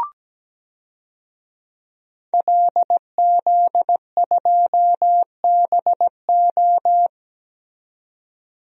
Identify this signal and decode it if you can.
Morse code, heard as a keyed tone, about 17 wpm, 700 Hz. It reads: LZ2BO
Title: DXCC-100 Call Signs - 5 Characters - 17wpm